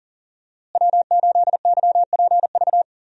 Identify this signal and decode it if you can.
Morse code, heard as a keyed tone, about 40 wpm, 700 Hz. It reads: W8YPV
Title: USA Call Signs - 5 Characters - Repeated 2x - 40wpm